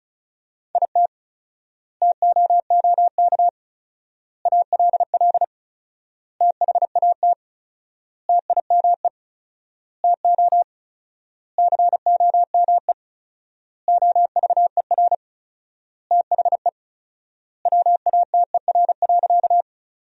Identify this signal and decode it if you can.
Morse code, heard as a keyed tone, about 34 wpm, 700 Hz. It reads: IT TOOK ALL THAT TIME TO COME OVER THE WATER.
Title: 4x Word-Spacing - Sentences from Top 300 Words - 35wpm